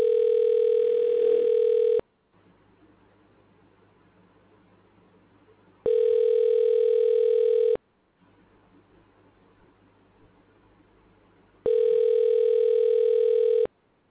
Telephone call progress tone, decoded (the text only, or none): ringback tone